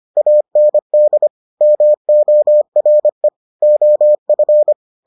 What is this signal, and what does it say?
Morse code, heard as a keyed tone, about 25 wpm, 600 Hz. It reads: AND MORE OF